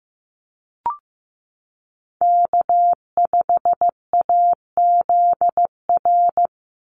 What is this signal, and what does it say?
Morse code, heard as a keyed tone, about 15 wpm, 700 Hz. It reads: K5AZR
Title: Call Signs Repeated 3x - 15wpm